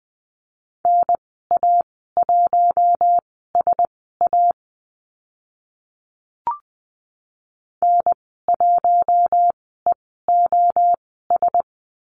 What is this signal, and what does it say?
Morse code, heard as a keyed tone, about 20 wpm, spaced as for 14 wpm, 700 Hz. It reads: NA1SA N1EOS
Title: 042 Lesson 8g - Number 1 - Call Signs 20wpm